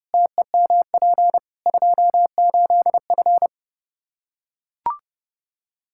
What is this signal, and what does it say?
Morse code, heard as a keyed tone, about 30 wpm, 700 Hz. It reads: TEMP 28F
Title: Practice QSO - Exchange 2 - Info - 30wpm